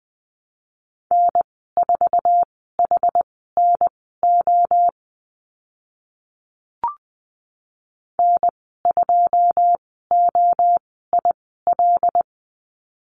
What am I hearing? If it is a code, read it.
Morse code, heard as a keyed tone, about 20 wpm, spaced as for 15 wpm, 700 Hz. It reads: N4HNO N2OIL
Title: 088 Lesson 15g - Number 5 - Call Signs 20wpm